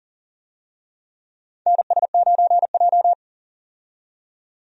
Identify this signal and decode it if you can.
Morse code, heard as a keyed tone, about 40 wpm, 690 Hz. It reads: NS9J
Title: USA Call Signs - 4 Characters - 40wpm